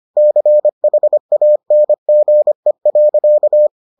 Morse code, heard as a keyed tone, about 25 wpm, 600 Hz. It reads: CHANGE.